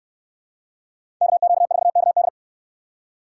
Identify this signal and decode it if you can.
Morse code, heard as a keyed tone, about 66 wpm, 700 Hz. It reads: KQ4KD